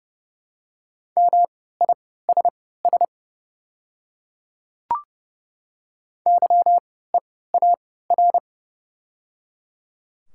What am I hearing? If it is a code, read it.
Morse code, heard as a keyed tone, about 29 wpm, spaced as for 17 wpm, 700 Hz. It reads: MISS YEAR
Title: 3x Character Spacing - Top 500 Words - 30wpm